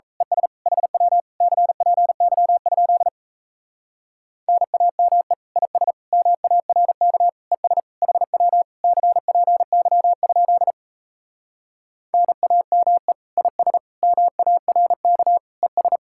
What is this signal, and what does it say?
Morse code, heard as a keyed tone, about 37 wpm, 700 Hz. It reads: ES HW CPY? NAME IS MARK ES HW CPY? NAME IS MARK ES